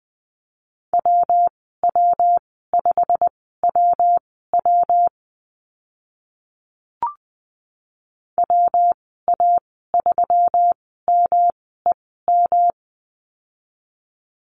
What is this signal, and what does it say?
Morse code, heard as a keyed tone, about 19 wpm, spaced as for 14 wpm, 700 Hz. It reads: WW5WW WA3MEM
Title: 122 Lesson 20g - Number 3 - Call Signs 20wpm